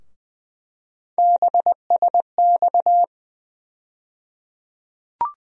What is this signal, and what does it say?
Morse code, heard as a keyed tone, about 20 wpm, 700 Hz. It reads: BSX
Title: USA Call Sign Postfixes - 3 Letters 20wpm